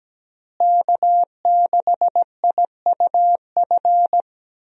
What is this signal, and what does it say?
Morse code, heard as a keyed tone, about 17 wpm, 700 Hz. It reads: K6IUF
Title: USA Call Signs - 5 Characters - Repeated 2x - 17wpm